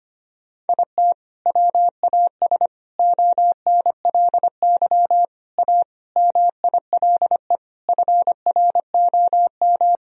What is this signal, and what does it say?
Morse code, heard as a keyed tone, about 25 wpm, 700 Hz. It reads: IT WAS ONLY A MILE FROM
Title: Sentences from Top 1000 Words - Review Part 1 - 25wpm